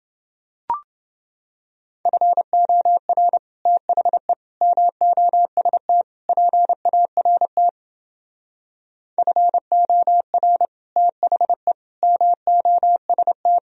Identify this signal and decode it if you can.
Morse code, heard as a keyed tone, about 30 wpm, 700 Hz. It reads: FOR THE MOST PART FOR THE MOST